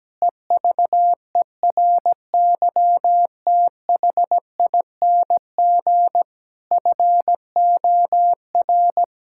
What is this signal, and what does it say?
Morse code, heard as a keyed tone, about 17 wpm, 700 Hz. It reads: EVERYTHING FOR
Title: Sentences from Top 1000 Words - Review Part 6 - 17wpm